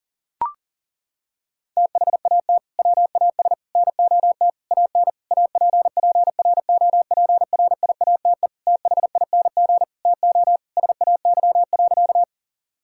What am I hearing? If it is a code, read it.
Morse code, heard as a keyed tone, about 40 wpm, 700 Hz. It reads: THAT WAS NOT AN APPROPRIATE THING TO SAY.
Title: Sentences from Top 1000 Words - Review Part 5 - 40wpm